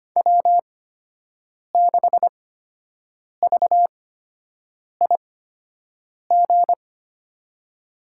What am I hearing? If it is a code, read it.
Morse code, heard as a keyed tone, about 25 wpm, spaced as for 7 wpm, 700 Hz. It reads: W6VIG
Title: USA Call Signs - 5 Characters - 8x Character Spacing - 25wpm